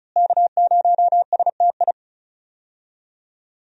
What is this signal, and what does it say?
Morse code, heard as a keyed tone, about 36 wpm, 700 Hz. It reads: K0STI
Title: USA Call Signs - 5 Characters - Repeated 2x - 35wpm